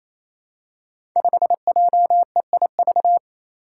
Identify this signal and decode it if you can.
Morse code, heard as a keyed tone, about 27 wpm, 700 Hz. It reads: 5JEIV
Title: Five Challenging Letter-Number Combinations 28wpm